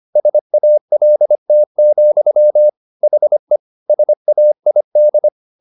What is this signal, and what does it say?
Morse code, heard as a keyed tone, about 25 wpm, 600 Hz. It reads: SALT, HE SAID
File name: the old man and the sea part 1 25wpm